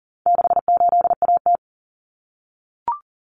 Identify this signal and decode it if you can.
Morse code, heard as a keyed tone, about 40 wpm, 700 Hz. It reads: 68AT
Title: Common Cut Numbers - 4-Digit 40wpm